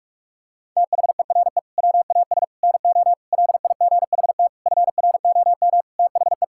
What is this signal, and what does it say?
Morse code, heard as a keyed tone, about 46 wpm, 700 Hz. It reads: THERE WAS NO LIGHT FROM THE
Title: Sentences from Top 1000 Words - Review Part 5 - 45wpm